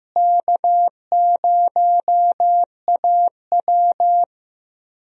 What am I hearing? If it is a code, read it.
Morse code, heard as a keyed tone, about 15 wpm, 700 Hz. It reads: K0AW
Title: USA Call Signs - 4 Characters - Repeated 3x - 15wpm